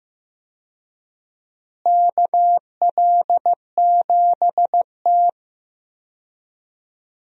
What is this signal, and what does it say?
Morse code, heard as a keyed tone, about 15 wpm, 700 Hz. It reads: KL7T
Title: USA Call Signs - 4 Characters - Repeated 2x - 15wpm